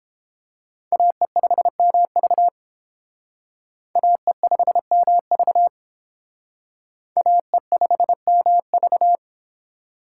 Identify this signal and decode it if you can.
Morse code, heard as a keyed tone, about 29 wpm, 700 Hz. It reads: AE5MV AE5MV AE5MV